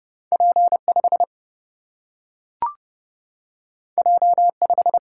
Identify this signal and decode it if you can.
Morse code, heard as a keyed tone, about 31 wpm, 700 Hz. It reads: P 5 J 5
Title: Single-Letter Single-Number 30wpm